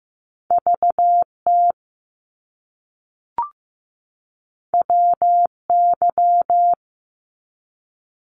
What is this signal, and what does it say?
Morse code, heard as a keyed tone, about 15 wpm, 700 Hz. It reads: VT WY
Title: US States 15wpm